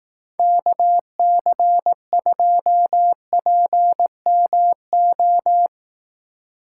Morse code, heard as a keyed tone, about 17 wpm, 700 Hz. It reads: KC2PMO
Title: Speed Racing - USA 2x3 Call Signs 15wpm